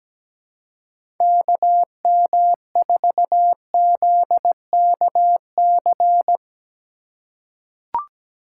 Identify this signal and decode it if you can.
Morse code, heard as a keyed tone, about 17 wpm, 700 Hz. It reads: KM4ZKC
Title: USA Call Signs - 6 Characters - 2x Character Spacing - 17wpm